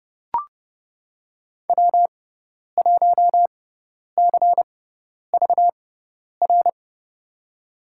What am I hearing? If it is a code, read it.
Morse code, heard as a keyed tone, about 30 wpm, spaced as for 10 wpm, 700 Hz. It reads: W1CVR
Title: USA Call Signs - 6x Character Spacing - 30wpm